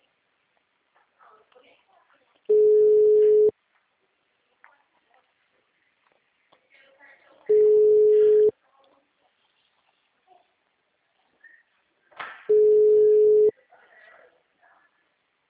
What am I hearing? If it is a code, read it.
Ringback tone (Europe, 425 Hz), with speech.
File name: out- -2007- - - .29557